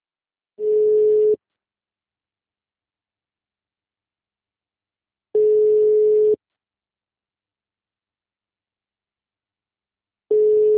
Ringback tone (Europe, 425 Hz).